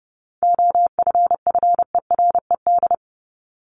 Morse code, heard as a keyed tone, about 30 wpm, 700 Hz. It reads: OFFERED
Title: ED Suffix - Top 100 Words with Base Words 30wpm